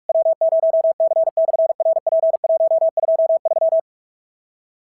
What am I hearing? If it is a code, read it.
Morse code, heard as a keyed tone, about 44 wpm, 650 Hz. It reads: W0C/RP123
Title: SOTA QSO Practice 45wpm